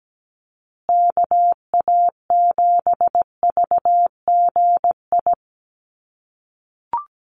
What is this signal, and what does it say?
Morse code, heard as a keyed tone, about 17 wpm, 700 Hz. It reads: KA7VGI
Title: USA Call Signs Repeated 2x 17wpm